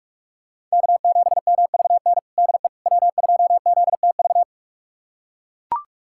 Morse code, heard as a keyed tone, about 41 wpm, 700 Hz. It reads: K7MVN DE W2ZTV
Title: USA Callsign DE Callsign 45wpm